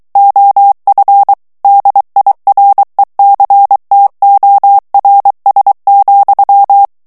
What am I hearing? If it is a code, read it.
Morse code, heard as a keyed tone, about 23 wpm, 800 Hz. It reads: OF DIRECTORS,
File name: ANGLAIS texte clair 3 vitesse 22 mots minute